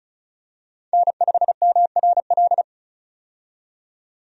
Morse code, heard as a keyed tone, about 34 wpm, 700 Hz. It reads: N5MRL
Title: USA Call Signs - 5 Characters - 35wpm